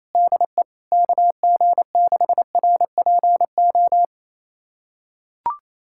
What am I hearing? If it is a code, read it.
Morse code, heard as a keyed tone, about 28 wpm, 700 Hz. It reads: DE KG6RPO
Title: USA Callsign DE Callsign 28wpm